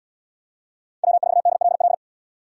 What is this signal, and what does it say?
Morse code, heard as a keyed tone, about 73 wpm, 700 Hz. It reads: W4DLL